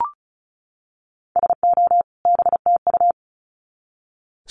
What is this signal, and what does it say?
Morse code, heard as a keyed tone, about 34 wpm, 700 Hz. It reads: SO BTU
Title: Common QSO Phrases 35wpm